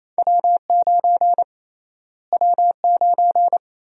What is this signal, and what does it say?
Morse code, heard as a keyed tone, about 28 wpm, 700 Hz. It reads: W9 W9
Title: USA Call Sign Prefixes - Repeated 2x - 28wpm